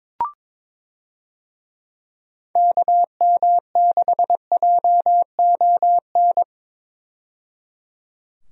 Morse code, heard as a keyed tone, about 22 wpm, 700 Hz. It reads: KM6JON
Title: USA Call Signs - 6 Characters - 22wpm